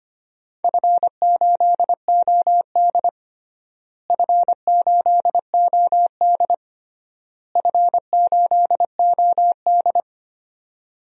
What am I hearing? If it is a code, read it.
Morse code, heard as a keyed tone, about 25 wpm, 700 Hz. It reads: F8OD F8OD F8OD
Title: DXCC-100 Call Signs - 4 Characters - Repeated 3x - 25wpm